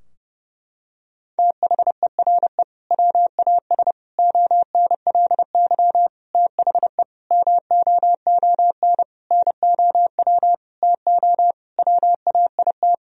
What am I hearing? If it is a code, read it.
Morse code, heard as a keyed tone, about 30 wpm, 700 Hz. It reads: THERE WAS ONLY THE MOON NOW TO WAIT
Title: Sentences from Top 1000 Words - Review Part 3 - 30wpm